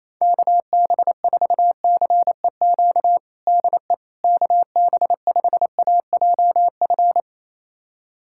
Morse code, heard as a keyed tone, about 28 wpm, 700 Hz. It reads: KB4CEQ DE KB5AJF
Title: USA Callsign DE Callsign 28wpm